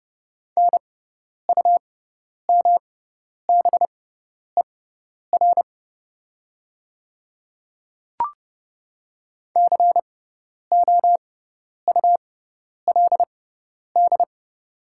Morse code, heard as a keyed tone, about 31 wpm, spaced as for 10 wpm, 700 Hz. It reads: NUMBER COULD
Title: 6x Character Spacing - Top 100 Words - 30wpm